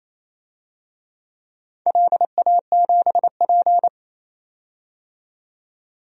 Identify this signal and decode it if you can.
Morse code, heard as a keyed tone, about 28 wpm, 700 Hz. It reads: LA7P
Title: DXCC-100 Call Signs - 4 Characters - 28wpm